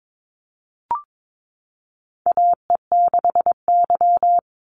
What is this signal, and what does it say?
Morse code, heard as a keyed tone, about 22 wpm, 700 Hz. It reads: AE6Y
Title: Call Signs Repeated 3x - 22wpm